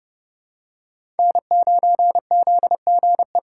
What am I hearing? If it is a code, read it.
Morse code, heard as a keyed tone, about 29 wpm, 700 Hz. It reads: N9ZGE
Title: USA Call Signs - 6x Character Spacing - 30wpm